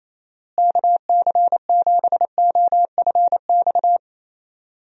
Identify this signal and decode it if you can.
Morse code, heard as a keyed tone, about 28 wpm, 700 Hz. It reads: KC7OFX
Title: USA Call Signs - 6 Characters - Repeated 3x - 28wpm